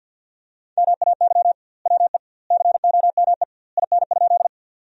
Morse code, heard as a keyed tone, about 50 wpm, 700 Hz. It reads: MAY WE COME IN?
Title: 2x Word-Spacing - Sentences from Top 100 Words - 50wpm